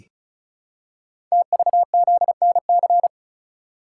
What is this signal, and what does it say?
Morse code, heard as a keyed tone, about 36 wpm, 700 Hz. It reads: TVZNC
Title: Code Groups - 5 Letter:Numbers - 35wpm